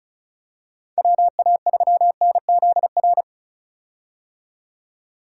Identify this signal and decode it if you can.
Morse code, heard as a keyed tone, about 36 wpm, 700 Hz. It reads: WA3NZR
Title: K1USN SST QSO Practice - 35wpm